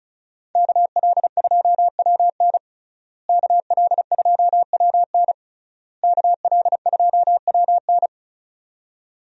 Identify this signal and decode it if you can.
Morse code, heard as a keyed tone, about 34 wpm, 700 Hz. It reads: KL2WN KL2WN KL2WN
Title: USA Call Signs - 5 Characters - Repeated 3x - 35wpm